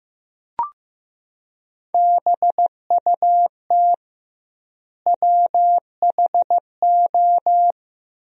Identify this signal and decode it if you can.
Morse code, heard as a keyed tone, about 15 wpm, 700 Hz. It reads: BUT WHO
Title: 2x Word-Spacing - Sentences from Top 100 Words - 15wpm